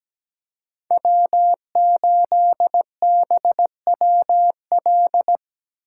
Morse code, heard as a keyed tone, about 20 wpm, 700 Hz. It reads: W8BWL
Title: Speed Racing - USA 1x3 Call Signs 17wpm